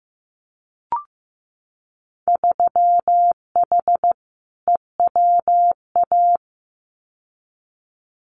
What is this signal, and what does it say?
Morse code, heard as a keyed tone, about 15 wpm, 700 Hz. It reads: 3H EWA
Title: Winter Field Day Exchanges Encore 15wpm